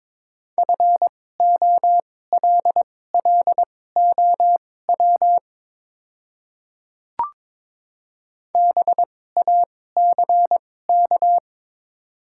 Morse code, heard as a keyed tone, about 22 wpm, spaced as for 16 wpm, 700 Hz. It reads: FOLLOW BACK
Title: 2x Character Spacing - Top 300 Words - 22wpm